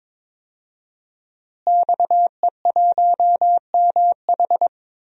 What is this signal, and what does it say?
Morse code, heard as a keyed tone, about 22 wpm, 700 Hz. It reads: XE1MH
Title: DXCC-100 Call Signs - 5 Characters - Repeated 3x - 22wpm